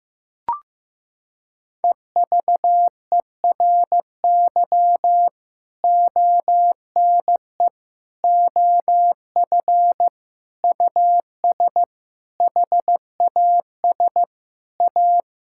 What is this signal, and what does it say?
Morse code, heard as a keyed tone, about 15 wpm, 700 Hz. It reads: EVERY ONE OF US HAS A
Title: Sentences from Top 1000 Words - Review Part 6 - 15wpm